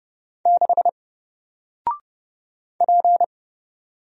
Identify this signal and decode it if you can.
Morse code, heard as a keyed tone, about 30 wpm, 700 Hz. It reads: 6 P
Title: Single Letter-Number 30wpm